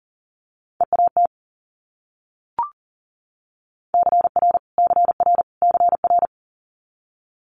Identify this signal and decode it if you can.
Morse code, heard as a keyed tone, about 37 wpm, 700 Hz. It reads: EAT CR CR CR
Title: Multi-letter Phonemes Repeated 3x 40wpm